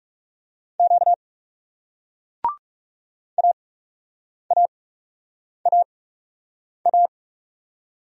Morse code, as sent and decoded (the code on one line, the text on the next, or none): --.- / .- .- .- .-
Q AAAA